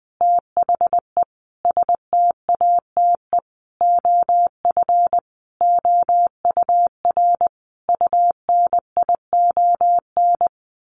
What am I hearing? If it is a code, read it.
Morse code, heard as a keyed tone, about 20 wpm, 700 Hz. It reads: THE STATE OF OUR UNION
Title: Sentences from Top 1400 Words 20wpm